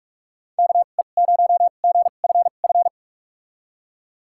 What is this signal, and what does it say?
Morse code, heard as a keyed tone, about 46 wpm, spaced as for 32 wpm, 700 Hz. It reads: KE0GFF
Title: USA Call Signs - 2x Character Spacing - 45wpm